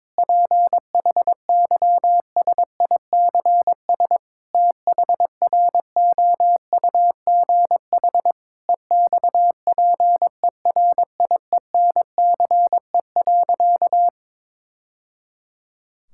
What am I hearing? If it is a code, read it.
Morse code, heard as a keyed tone, about 22 wpm, 700 Hz. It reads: PHYSICS THROUGH EXPERIENCE.
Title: Sentences from Top 1000 Words - Review Part 5 - 22wpm